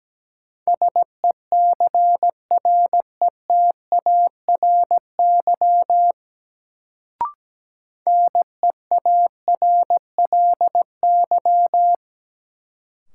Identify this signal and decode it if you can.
Morse code, heard as a keyed tone, about 17 wpm, 700 Hz. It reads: SECRETARY NEARLY
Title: Sentences from Top 1300 Words - New Word Drill Alt - 17wpm